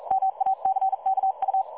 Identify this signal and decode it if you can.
Morse code, heard as a keyed tone, about 42 wpm, 770 Hz. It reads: MACKU
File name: dlazaawansowanych / nagraniazpasma / websdr recording 3565.0kHz